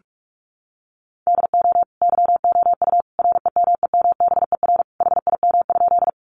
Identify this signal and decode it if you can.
Morse code, heard as a keyed tone, about 44 wpm, 700 Hz. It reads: DO YOU REMEMBER HIM?
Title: Sentences from Top 400 Words 45wpm